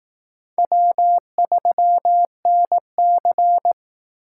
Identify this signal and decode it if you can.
Morse code, heard as a keyed tone, about 19 wpm, 700 Hz. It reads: W3NC